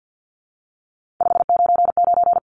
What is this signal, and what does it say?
Morse code, heard as a keyed tone, about 50 wpm, 700 Hz. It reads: 599
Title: Common QSO Phrases 50wpm